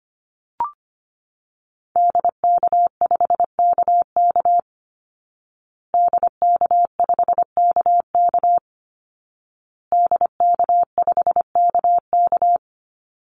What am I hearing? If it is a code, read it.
Morse code, heard as a keyed tone, about 25 wpm, 700 Hz. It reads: DK5KK DK5KK DK5KK
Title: Call Signs Repeated 2x - 25wpm